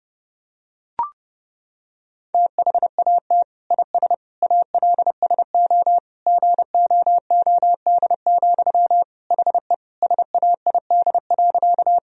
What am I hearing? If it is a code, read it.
Morse code, heard as a keyed tone, about 32 wpm, 700 Hz. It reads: THAT IS ALSO GOOD, HE SAID.
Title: Sentences from Top 1000 Words - Review Part 1 - 30wpm